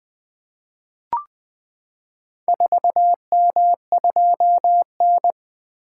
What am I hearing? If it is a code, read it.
Morse code, heard as a keyed tone, about 20 wpm, 700 Hz. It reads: 4M2N
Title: Four Letter-Number Combinations - 20wpm